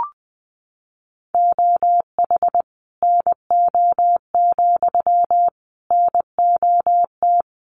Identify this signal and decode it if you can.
Morse code, heard as a keyed tone, about 19 wpm, 700 Hz. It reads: OH NO, NOT
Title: Sentences from Top 1300 Words 20wpm